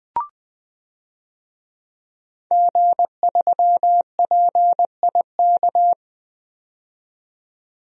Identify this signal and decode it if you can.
Morse code, heard as a keyed tone, about 20 wpm, 700 Hz. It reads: G3PIK
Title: DXCC-100 Call Signs - 5 Characters - 20wpm